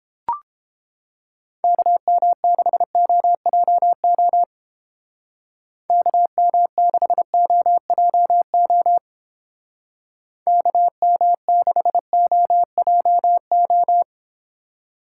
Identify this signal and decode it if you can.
Morse code, heard as a keyed tone, about 28 wpm, 700 Hz. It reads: KM6OJO KM6OJO KM6OJO